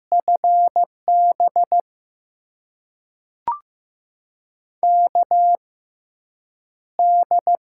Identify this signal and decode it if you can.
Morse code, heard as a keyed tone, about 15 wpm, spaced as for 8 wpm, 700 Hz. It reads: FB KD
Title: USA Call Signs - 6 Characters - 6x Character Spacing - 15wpm